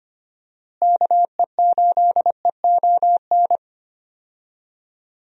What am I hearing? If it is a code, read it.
Morse code, heard as a keyed tone, about 25 wpm, 700 Hz. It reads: KE8EON
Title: USA Call Signs - 6 Characters - Repeated 2x - 25wpm